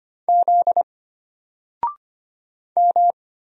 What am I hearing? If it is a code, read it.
Morse code, heard as a keyed tone, about 25 wpm, 700 Hz. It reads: Z M
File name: Single-Letter-Number-25wpm